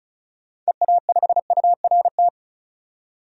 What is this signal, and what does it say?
Morse code, heard as a keyed tone, about 36 wpm, 700 Hz. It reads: EA5URT
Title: DXCC-100 Call Signs - 6 Characters - 35wpm